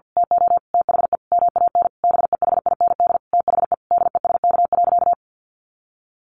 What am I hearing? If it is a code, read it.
Morse code, heard as a keyed tone, about 50 wpm, 700 Hz. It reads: TO THE MAN BEHIND THE DESK.
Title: A Sound Of Thunder 50wpm